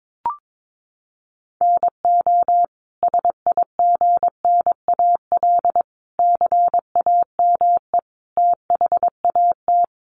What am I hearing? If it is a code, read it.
Morse code, heard as a keyed tone, about 22 wpm, 700 Hz. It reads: NO SIGNAL CAME THAT
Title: Sentences from Top 1400 Words 22wpm